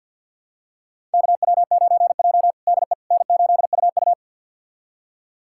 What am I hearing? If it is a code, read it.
Morse code, heard as a keyed tone, about 49 wpm, 700 Hz. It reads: KW9J DE N8UU
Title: USA Callsign DE Callsign - 4 Characters - 50wpm